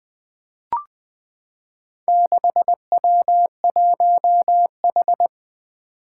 Morse code, heard as a keyed tone, about 20 wpm, 700 Hz. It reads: 6W1H
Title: Four Challenging Letter-Number Combinations 20wpm